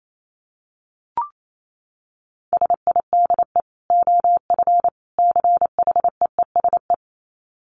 Morse code, heard as a keyed tone, about 28 wpm, 700 Hz. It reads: SIDE OF CHEESE
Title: Dit Challenge 28wpm